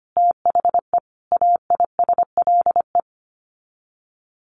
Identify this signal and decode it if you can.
Morse code, heard as a keyed tone, about 25 wpm, 700 Hz. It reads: THE AISLE
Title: Sets of 5 Words Encore2 25wpm